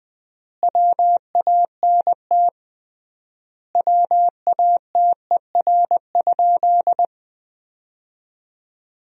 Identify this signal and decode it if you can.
Morse code, heard as a keyed tone, about 20 wpm, 700 Hz. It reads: WANT WATER?
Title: 3x Word-Spacing - Sentences from Top 200 Words - 20wpm